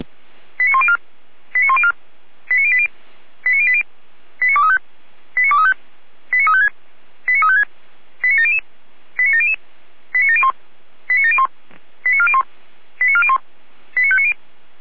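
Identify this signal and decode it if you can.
Five-tone selective call (ZVEI), heard as quick runs of five tones, tones of about 80 ms. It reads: 89104 89104 89080 89080 89235 89235 89356 89356 89700 89700 89701 89701 89401 89401 89590
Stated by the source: ZVEI five-tone selective call received with an SDR.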